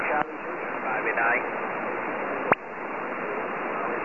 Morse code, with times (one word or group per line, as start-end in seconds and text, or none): none